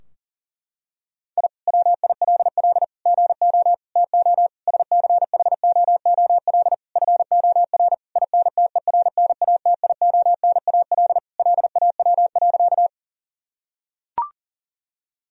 Morse code, heard as a keyed tone, about 40 wpm, 700 Hz. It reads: I WILL GO TO SCHOOL FOR INTERNATIONAL LAW.
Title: Sentences from Top 1000 Words - Review Part 4 - 40wpm